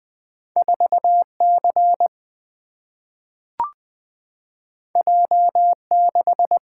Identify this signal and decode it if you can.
Morse code, heard as a keyed tone, about 20 wpm, 700 Hz. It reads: 4C J6
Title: Single-Letter Single-Number 20wpm